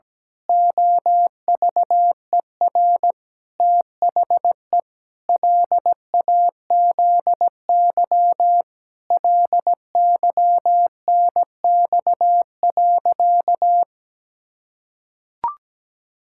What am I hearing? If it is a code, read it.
Morse code, heard as a keyed tone, about 17 wpm, 700 Hz. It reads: OVER THE LAZY LYNX.
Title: Pangram Challenge v2 - 17wpm